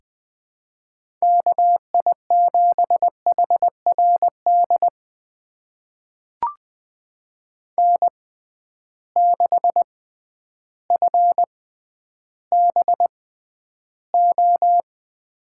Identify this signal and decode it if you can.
Morse code, heard as a keyed tone, about 20 wpm, spaced as for 11 wpm, 700 Hz. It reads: KI7HRD N6FBO
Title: USA Call Signs - 6x Character Spacing - 20wpm